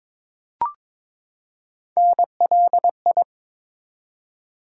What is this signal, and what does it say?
Morse code, heard as a keyed tone, about 23 wpm, 700 Hz. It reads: NLI
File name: ARRL Sections v2 - 22wpm